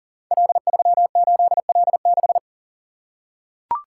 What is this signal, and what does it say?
Morse code, heard as a keyed tone, about 40 wpm, 700 Hz. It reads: L38L6
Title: Code Groups - 5 Letter:Numbers - 40wpm